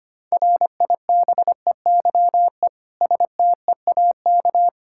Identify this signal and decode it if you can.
Morse code, heard as a keyed tone, about 25 wpm, 700 Hz. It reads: RIBEYE STEAK
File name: practice / 25wpm / Difficult Words with Hints - 25wpm